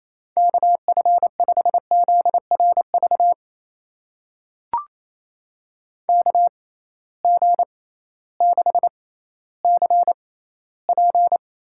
Morse code, heard as a keyed tone, about 28 wpm, spaced as for 15 wpm, 700 Hz. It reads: KF5ZRV KG6CP
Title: USA Call Signs - 6 Characters - 6x Character Spacing - 28wpm